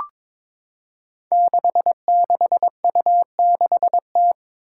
Morse code, heard as a keyed tone, about 22 wpm, 700 Hz. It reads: 66U6T
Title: Five Challenging Letter-Number Combinations 22wpm